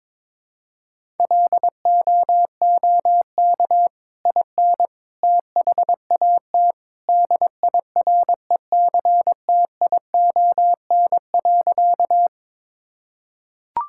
Morse code, heard as a keyed tone, about 22 wpm, 700 Hz. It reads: LOOK IN THAT DIRECTION.
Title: Sentences from Top 1000 Words - Review Part 5 - 22wpm